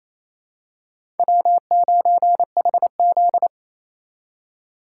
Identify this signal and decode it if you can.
Morse code, heard as a keyed tone, about 28 wpm, 700 Hz. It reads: W9HZ
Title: USA Call Signs - 4 Characters - 2x Character Spacing - 28wpm